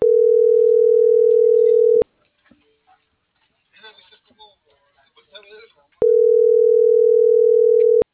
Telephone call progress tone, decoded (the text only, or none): ringback tone